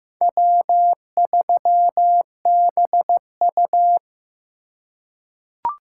Morse code, heard as a keyed tone, about 15 wpm, 700 Hz. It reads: W3BU
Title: USA Call Signs - 4 Characters - 2x Character Spacing - 15wpm